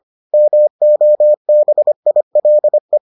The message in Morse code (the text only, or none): MOBILE